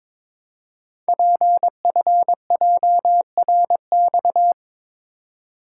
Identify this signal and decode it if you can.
Morse code, heard as a keyed tone, about 22 wpm, 700 Hz. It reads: PFJRX
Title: Code Groups - 5 Letters - 22wpm